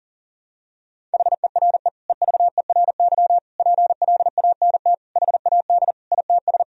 Morse code, heard as a keyed tone, about 37 wpm, 700 Hz. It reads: HERE EVERY PLANT HAD ITS
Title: Sentences from Top 1000 Words - Review Part 1 - 40wpm